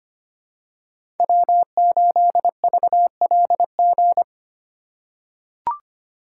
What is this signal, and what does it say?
Morse code, heard as a keyed tone, about 25 wpm, 700 Hz. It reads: W8VLG
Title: USA Call Signs - 3x Character Spacing - 25wpm